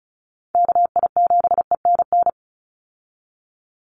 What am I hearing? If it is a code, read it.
Morse code, heard as a keyed tone, about 34 wpm, 700 Hz. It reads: KI7ENN
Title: USA Call Signs 35wpm